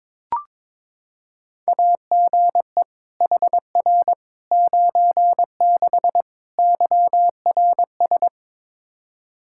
Morse code, heard as a keyed tone, about 22 wpm, 700 Hz. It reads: AGE HR 96 YRS
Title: Practice QSO - Exchange 3 - More Info - 22wpm